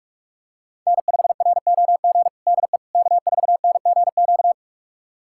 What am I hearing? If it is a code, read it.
Morse code, heard as a keyed tone, about 42 wpm, 700 Hz. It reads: N5ROG DE K4NGQ
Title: USA Callsign DE Callsign - 5 Characters - 45wpm